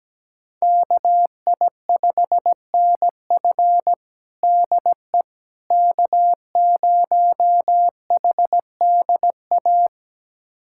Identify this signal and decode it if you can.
Morse code, heard as a keyed tone, about 17 wpm, 700 Hz. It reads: KI5NF DE K0HDA
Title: USA Callsign DE Callsign - 5 Characters - 17wpm